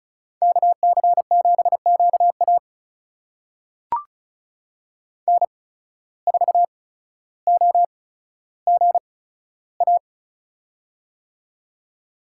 Morse code, heard as a keyed tone, about 36 wpm, spaced as for 15 wpm, 700 Hz. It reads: KC7QA N4OGA
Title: USA Call Signs - 5 Characters - 8x Character Spacing - 35wpm